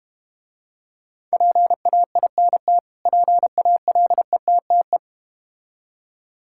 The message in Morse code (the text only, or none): PAINT PALETTE